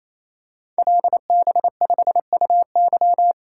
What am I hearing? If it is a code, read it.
Morse code, heard as a keyed tone, about 28 wpm, 700 Hz. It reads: LB5UY
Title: Code Groups - 5 Letter:Numbers - 28wpm